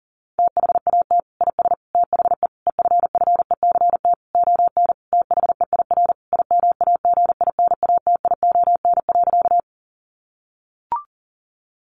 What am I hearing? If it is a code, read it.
Morse code, heard as a keyed tone, about 40 wpm, 700 Hz. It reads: THAT IS THE EFFECT ON THEIR IMAGINATION.
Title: Sentences from Top 1300 Words 40wpm